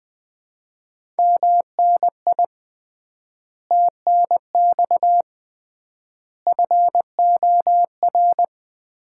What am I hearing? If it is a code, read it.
Morse code, heard as a keyed tone, about 20 wpm, 700 Hz. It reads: MNI TNX FOR
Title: Common QSO Phrases - 3x Word-Spacing - 20wpm